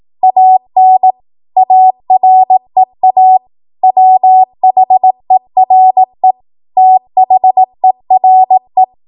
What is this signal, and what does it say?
Morse code, heard as a keyed tone, about 18 wpm, 750 Hz. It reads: AN AREA WHERE THERE
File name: Morse / 18WPM